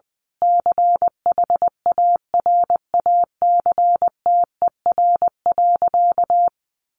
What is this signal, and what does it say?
Morse code, heard as a keyed tone, about 20 wpm, 700 Hz. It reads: CHARACTER.
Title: Sentences from Top 1200 Words 20wpm